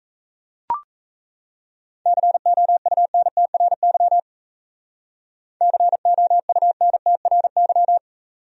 Morse code, heard as a keyed tone, about 35 wpm, 700 Hz. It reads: COUNTRY COUNTRY
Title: Speed-Racing Top 500 Words 28wpm